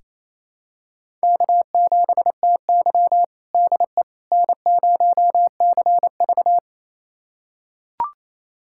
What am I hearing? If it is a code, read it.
Morse code, heard as a keyed tone, about 27 wpm, 700 Hz. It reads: K7TY DE N0CV
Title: USA Callsign DE Callsign - 4 Characters - 28wpm